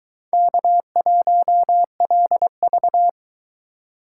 Morse code, heard as a keyed tone, about 22 wpm, 700 Hz. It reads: K1LV